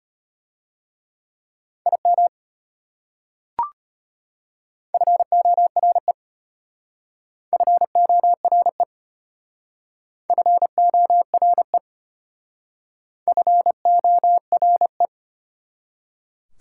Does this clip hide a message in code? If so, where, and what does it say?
Yes: Morse code from 1.9 to 15.1 s, heard as a keyed tone, about 33 wpm, 700 Hz. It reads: IM FORE FORE FORE FORE